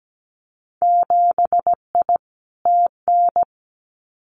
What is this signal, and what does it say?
Morse code, heard as a keyed tone, about 17 wpm, 700 Hz. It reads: 7I TN
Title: Winter Field Day Exchanges Encore 17wpm